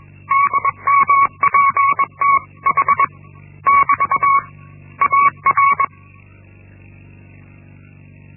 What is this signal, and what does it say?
Morse code, heard as a keyed tone, about 21 wpm, 1120 Hz. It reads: DMPTH <BT> AR
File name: Linksmakalnis-1